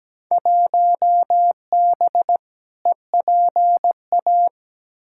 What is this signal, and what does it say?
Morse code, heard as a keyed tone, about 17 wpm, 700 Hz. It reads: 1B EPA
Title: Field Day Exchanges Encore - 17wpm